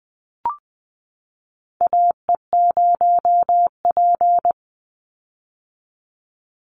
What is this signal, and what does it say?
Morse code, heard as a keyed tone, about 20 wpm, 700 Hz. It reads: AE0P
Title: Call Signs - 20wpm